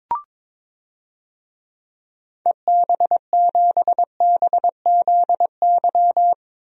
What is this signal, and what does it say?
Morse code, heard as a keyed tone, about 22 wpm, 700 Hz. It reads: EB7BZY
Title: DXCC-100 Call Signs - 6 Characters - 22wpm